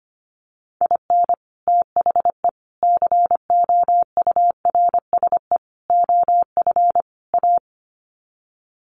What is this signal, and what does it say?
Morse code, heard as a keyed tone, about 25 wpm, 700 Hz. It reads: IN THE COURSE OF A
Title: Sets of 5 Words Encore2 25wpm